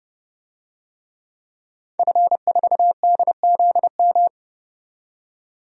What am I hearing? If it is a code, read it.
Morse code, heard as a keyed tone, about 29 wpm, 700 Hz. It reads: F4DZM
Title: DXCC-100 Call Signs - 5 Characters - 30wpm